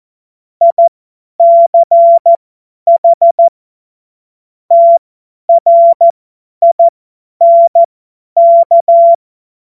Morse code, heard as a keyed tone, about 14 wpm, spaced as for 10 wpm, 680 Hz. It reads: ICH TRINK
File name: Morsekurs / Witze / Witz 27. Juli 2021, 50 F70